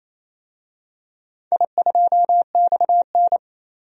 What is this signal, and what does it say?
Morse code, heard as a keyed tone, about 29 wpm, 700 Hz. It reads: I2XN
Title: DXCC-100 Call Signs - 4 Characters - 28wpm